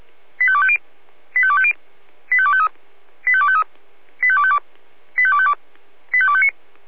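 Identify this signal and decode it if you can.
Five-tone selective call (ZVEI), heard as quick runs of five tones, tones of about 70 ms. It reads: 85250 85250 85253 85253 85252 85252 85259
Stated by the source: ZVEI five-tone selective call 85248 received on an SDR.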